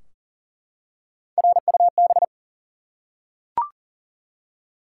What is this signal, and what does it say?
Morse code, heard as a keyed tone, about 38 wpm, 700 Hz. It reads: RUB
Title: Three Letter Words - 40wpm